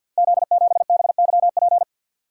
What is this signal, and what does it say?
Morse code, heard as a keyed tone, about 50 wpm, 700 Hz. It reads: Z7BYP
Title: Code Groups - 5 Letter:Numbers - 50wpm